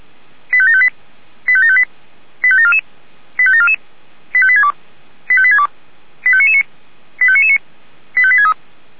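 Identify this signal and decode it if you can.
Five-tone selective call (ZVEI), heard as quick runs of five tones, tones of about 70 ms. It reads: 85758 85758 85744 85744 85862 85862 85999 85999 85863
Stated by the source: ZVEI five-tone selective call 85861 received on SDR, 464 channel.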